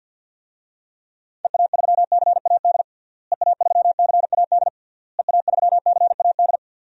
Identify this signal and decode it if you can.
Morse code, heard as a keyed tone, about 52 wpm, 700 Hz. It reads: EA3CAD EA3CAD EA3CAD
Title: DXCC-100 Call Signs - 6 Characters - Repeated 3x - 50wpm